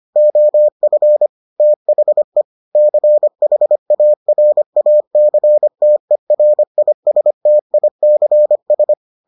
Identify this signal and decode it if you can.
Morse code, heard as a keyed tone, about 25 wpm, 600 Hz. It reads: OF THE CHARACTERISTICS